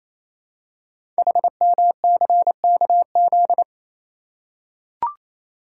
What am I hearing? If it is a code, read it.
Morse code, heard as a keyed tone, about 28 wpm, 700 Hz. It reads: HMCKZ
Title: Code Groups - 5 Letters - 28wpm